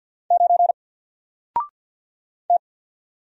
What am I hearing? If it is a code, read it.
Morse code, heard as a keyed tone, about 50 wpm, 700 Hz. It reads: 9 T
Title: Single Character (Letter:Number:Punctuation:Prosign) 50wpm